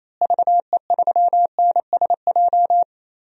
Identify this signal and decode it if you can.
Morse code, heard as a keyed tone, about 28 wpm, 700 Hz. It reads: VE3NSJ
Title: DXCC-100 Call Signs - 6 Characters - Repeated 3x - 28wpm